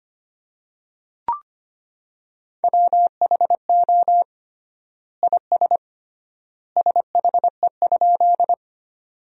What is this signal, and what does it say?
Morse code, heard as a keyed tone, about 25 wpm, 700 Hz. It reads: WHO IS SHE?
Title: 3x Word-Spacing - Sentences from Top 100 Words - 25wpm